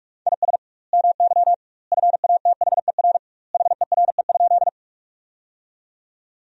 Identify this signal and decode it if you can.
Morse code, heard as a keyed tone, about 47 wpm, 700 Hz. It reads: IS MY FATHER HERE?
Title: 2x Word-Spacing - Sentences from Top 300 Words - 45wpm